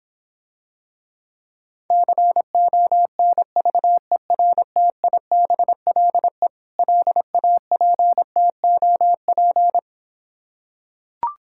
Morse code, heard as a keyed tone, about 26 wpm, 700 Hz. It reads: CONVERTIBLE LAPTOP